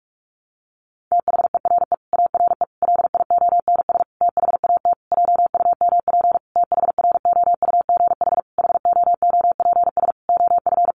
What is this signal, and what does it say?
Morse code, heard as a keyed tone, about 44 wpm, 700 Hz. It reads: THERE ARE LIONS THAT JUMP THROUGH HOOPS OF
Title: Easy Phrases 45wpm